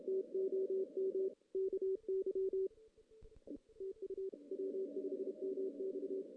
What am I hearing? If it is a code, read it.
Morse code, heard as a keyed tone, about 32 wpm, 380 Hz. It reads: TOM KY TU W5MX
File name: 2016 JAN CW NAQP / 2016 Jan naqp k5ot - 0-2